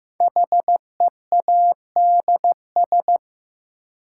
Morse code, heard as a keyed tone, about 15 wpm, 700 Hz. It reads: HEADS
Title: Sentences from Top 1000 Words - Review Part 3 - New Word Drill - 15wpm